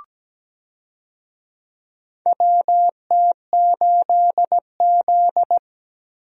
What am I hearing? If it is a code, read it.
Morse code, heard as a keyed tone, about 17 wpm, 700 Hz. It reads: WT8Z
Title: USA Call Signs - 4 Characters - Repeated 3x - 17wpm